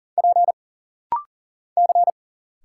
Morse code, heard as a keyed tone, about 40 wpm, 700 Hz. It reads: P C
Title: Single Letter-Number Alt - Rapid-Fire 40wpm